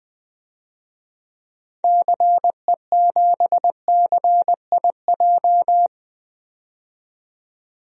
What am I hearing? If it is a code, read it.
Morse code, heard as a keyed tone, about 20 wpm, 700 Hz. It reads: CE7CIJ
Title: DXCC-100 Call Signs - 6 Characters - 20wpm